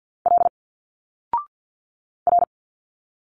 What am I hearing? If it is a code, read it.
Morse code, heard as a keyed tone, about 50 wpm, 700 Hz. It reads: L R
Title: Single Letters 50wpm